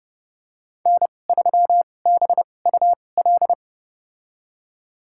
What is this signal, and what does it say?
Morse code, heard as a keyed tone, about 31 wpm, spaced as for 22 wpm, 700 Hz. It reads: N3BUL
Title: USA Call Signs - 5 Characters - 2x Character Spacing - 30wpm